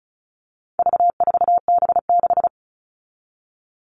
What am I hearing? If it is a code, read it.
Morse code, heard as a keyed tone, about 34 wpm, 700 Hz. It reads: V4B6
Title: All Cut Numbers - 4-Digit 35wpm